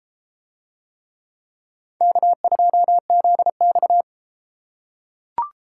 Morse code, heard as a keyed tone, about 31 wpm, 700 Hz. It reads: K2ZX